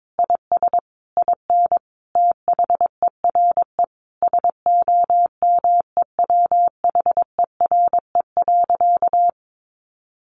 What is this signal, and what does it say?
Morse code, heard as a keyed tone, about 22 wpm, 700 Hz. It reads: IS IN THERE SOMEWHERE.
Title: Sentences from Top 1400 Words 22wpm